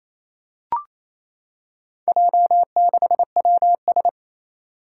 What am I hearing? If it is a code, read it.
Morse code, heard as a keyed tone, about 28 wpm, 700 Hz. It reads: J6WS
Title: Four Challenging Letter-Number Combinations 28wpm